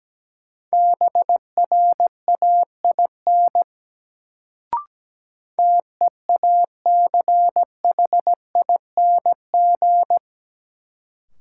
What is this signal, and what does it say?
Morse code, heard as a keyed tone, about 17 wpm, 700 Hz. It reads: BRAIN TEACHING
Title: Sentences from Top 1100 Words - New Word Drill Alt - 17wpm